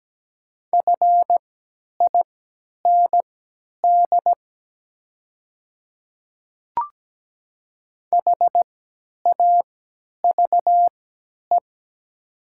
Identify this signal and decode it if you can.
Morse code, heard as a keyed tone, about 17 wpm, spaced as for 10 wpm, 700 Hz. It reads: FIND HAVE
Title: 3x Character Spacing - Top 100 Words - 17wpm